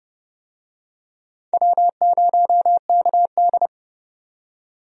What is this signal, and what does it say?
Morse code, heard as a keyed tone, about 30 wpm, 700 Hz. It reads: W0KD
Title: USA Call Signs - 4 Characters - 30wpm